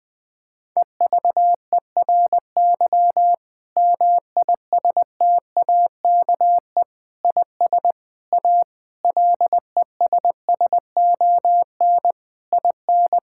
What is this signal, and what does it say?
Morse code, heard as a keyed tone, about 20 wpm, 700 Hz. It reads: EVERY MISTAKE IS A LESSON IN
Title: Difficult Sentences with Hints - 20wpm